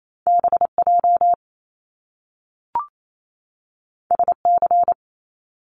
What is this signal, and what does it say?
Morse code, heard as a keyed tone, about 28 wpm, 700 Hz. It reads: BJ SC
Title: Top Two Letter Combinations in Words 28wpm